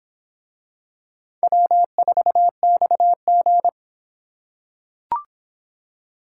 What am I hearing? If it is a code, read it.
Morse code, heard as a keyed tone, about 24 wpm, 700 Hz. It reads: W4XG